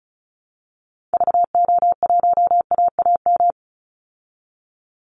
Morse code, heard as a keyed tone, about 34 wpm, 700 Hz. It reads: VO1AAM
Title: Call Signs Repeated 2x - 35wpm